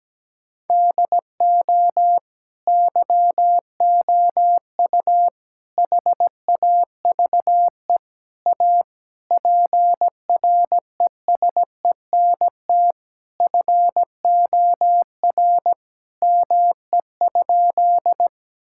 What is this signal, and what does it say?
Morse code, heard as a keyed tone, about 17 wpm, 700 Hz. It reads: DO YOU HAVE A PRESENT FOR ME?
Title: Sentences from Top 1000 Words - Review Part 5 - 17wpm